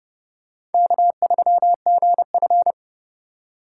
Morse code, heard as a keyed tone, about 31 wpm, 700 Hz. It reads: K3GF